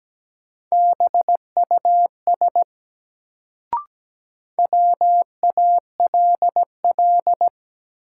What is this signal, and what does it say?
Morse code, heard as a keyed tone, about 17 wpm, 700 Hz. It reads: BUS WALL
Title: Sentences from Top 1000 Words - Review Part 6 - New Word Drill - 17wpm